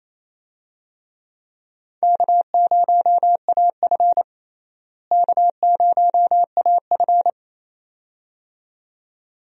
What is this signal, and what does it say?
Morse code, heard as a keyed tone, about 28 wpm, 700 Hz. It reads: K0AF K0AF
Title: USA Call Signs - 4 Characters - Repeated 2x - 28wpm